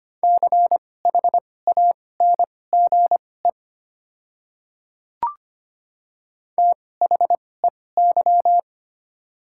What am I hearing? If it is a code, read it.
Morse code, heard as a keyed tone, about 25 wpm, spaced as for 18 wpm, 700 Hz. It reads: CHANGE THEY
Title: 2x Character Spacing - Top 200 Words - 25wpm